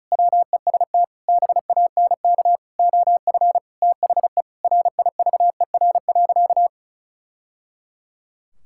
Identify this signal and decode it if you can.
Morse code, heard as a keyed tone, about 36 wpm, 700 Hz. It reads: WEST BANK OF THE RIVER.
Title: Sentences from Top 1000 Words - Review Part 6 - 35wpm